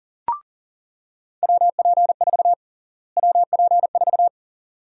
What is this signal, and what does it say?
Morse code, heard as a keyed tone, about 40 wpm, 700 Hz. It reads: WP4 WP4
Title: USA Call Sign Prefixes - Repeated 2x - 40wpm